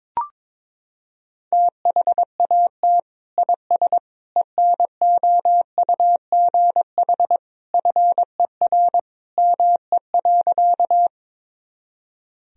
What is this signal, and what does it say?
Morse code, heard as a keyed tone, about 22 wpm, 700 Hz. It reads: THAT IS ENOUGH FER ME.
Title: Sentences from Top 400 Words and Abbreviations 22wpm